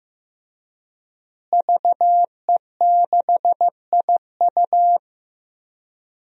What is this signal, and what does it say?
Morse code, heard as a keyed tone, about 15 wpm, 700 Hz. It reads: VE6IU
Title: DXCC-100 Call Signs - 5 Characters - Repeated 3x - 15wpm